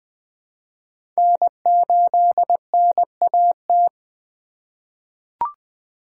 Morse code, heard as a keyed tone, about 20 wpm, 700 Hz. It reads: N8NAT
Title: USA Call Signs - 5 Characters - 3x Character Spacing - 20wpm